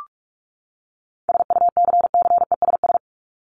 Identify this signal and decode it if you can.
Morse code, heard as a keyed tone, about 44 wpm, 700 Hz. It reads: SUCCESS
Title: Top 751-1000 Words 45wpm